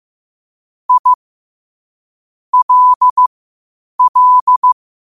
Morse code, heard as a keyed tone, about 15 wpm, spaced as for 8 wpm, 1000 Hz. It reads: I LL
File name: qotdfiles / 8wpm